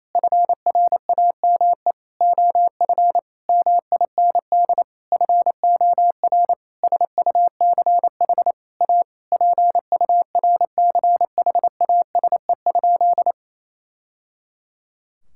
Morse code, heard as a keyed tone, about 28 wpm, 700 Hz. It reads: FRAME OF MIND FOR SUCH A PURCHASE?
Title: Sentences from Top 1000 Words - Review Part 6 - 28wpm